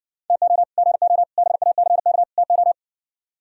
Morse code, heard as a keyed tone, about 59 wpm, 700 Hz. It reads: TO GO BACK TO